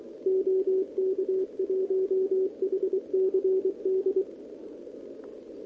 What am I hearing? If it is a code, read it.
Morse code, heard as a keyed tone, about 25 wpm, 410 Hz. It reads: OK1HCD
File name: OK QRP 2026 / OKQRP2026